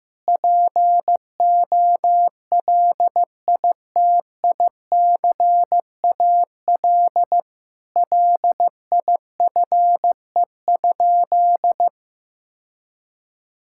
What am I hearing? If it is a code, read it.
Morse code, heard as a keyed tone, about 15 wpm, 690 Hz. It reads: POLITICAL LIFE?
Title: Sentences from Top 1000 Words - Review Part 5 - 15wpm